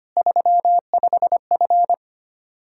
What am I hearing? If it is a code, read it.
Morse code, heard as a keyed tone, about 25 wpm, 700 Hz. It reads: 35F
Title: Practice QSO - Exchange 2 - Info - 25wpm